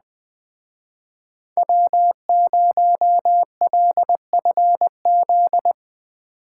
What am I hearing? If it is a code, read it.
Morse code, heard as a keyed tone, about 20 wpm, 700 Hz. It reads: W0LFZ
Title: Speed Racing - USA 1x3 Call Signs 17wpm